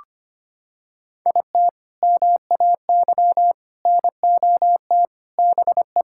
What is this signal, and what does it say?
Morse code, heard as a keyed tone, about 25 wpm, 700 Hz. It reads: IT MAY NOT BE
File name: practice / 25wpm / Sentences with 3-Word Phrases - 25wpm